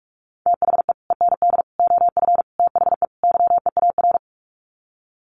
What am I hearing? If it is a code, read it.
Morse code, heard as a keyed tone, about 44 wpm, 700 Hz. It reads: THE END OF THE YEAR
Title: Sets of 5 Words 45wpm